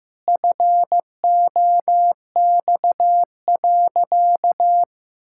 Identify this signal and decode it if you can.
Morse code, heard as a keyed tone, about 16 wpm, 700 Hz. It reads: FOX.
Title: Pangram Challenge v2 - 15wpm